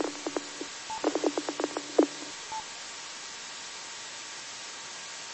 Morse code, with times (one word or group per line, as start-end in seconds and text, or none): none